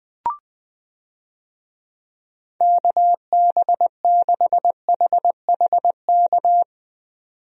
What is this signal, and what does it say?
Morse code, heard as a keyed tone, about 20 wpm, 700 Hz. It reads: KB6HHK
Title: USA Call Signs - 6 Characters - 20wpm